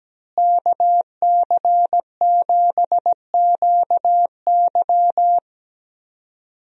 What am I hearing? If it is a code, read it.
Morse code, heard as a keyed tone, about 17 wpm, 700 Hz. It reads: KC7QY
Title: USA Call Signs - 5 Characters - Repeated 2x - 17wpm